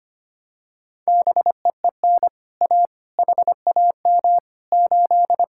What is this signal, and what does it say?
Morse code, heard as a keyed tone, about 25 wpm, 700 Hz. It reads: BEEN A HAM 8
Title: Speed-Racing Common QSO Phrases - 25wpm